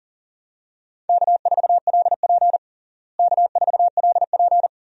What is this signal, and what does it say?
Morse code, heard as a keyed tone, about 37 wpm, 700 Hz. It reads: K4LP K4LP
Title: USA Call Signs - 4 Characters - Repeated 2x - 40wpm